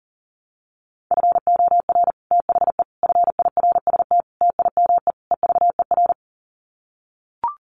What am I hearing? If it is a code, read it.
Morse code, heard as a keyed tone, about 40 wpm, 700 Hz. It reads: FOR THE FIRST TIME EVER
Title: Sets of 5 Words Encore2 40wpm